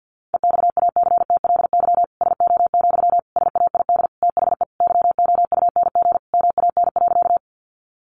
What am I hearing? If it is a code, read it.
Morse code, heard as a keyed tone, about 50 wpm, 700 Hz. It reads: EXACTLY SO, SAID THE YOUNG MAN.
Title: Sentences from Top 900 Words 50wpm